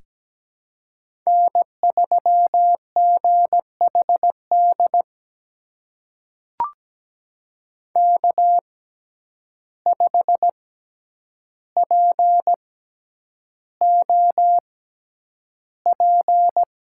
Morse code, heard as a keyed tone, about 17 wpm, spaced as for 9 wpm, 700 Hz. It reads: N3GHD K5POP
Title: USA Call Signs - 5 Characters - 6x Character Spacing - 17wpm